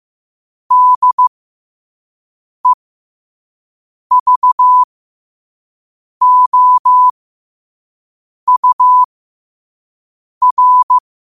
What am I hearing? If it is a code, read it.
Morse code, heard as a keyed tone, about 15 wpm, spaced as for 5 wpm, 1000 Hz. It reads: DEVOUR
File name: qotdfiles / 5wpm